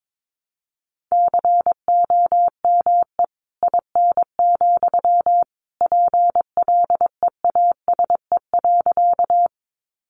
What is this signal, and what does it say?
Morse code, heard as a keyed tone, about 22 wpm, 700 Hz. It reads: COME IN, PLEASE.
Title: Sentences from Top 1500 Words 22wpm